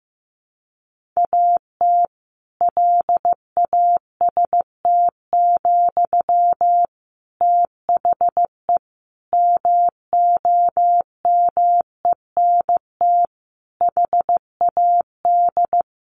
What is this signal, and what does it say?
Morse code, heard as a keyed tone, about 15 wpm, 700 Hz. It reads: AT LAST, THE MOMENT HAD
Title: Sentences from Top 1500 Words 15wpm